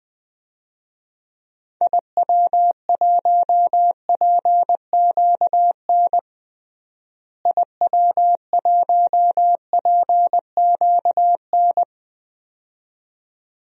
Morse code, heard as a keyed tone, about 20 wpm, 700 Hz. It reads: IW1PQN IW1PQN
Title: DXCC-100 Call Signs - 6 Characters - Repeated 2x - 20wpm